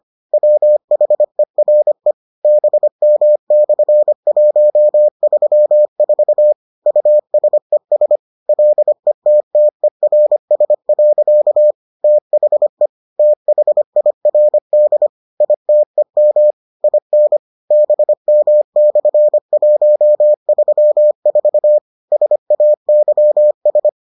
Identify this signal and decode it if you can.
Morse code, heard as a keyed tone, about 25 wpm, 600 Hz. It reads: WHERE BM/134 USES LETTERS. THE THIRD ITEM IN BM/134 SAYS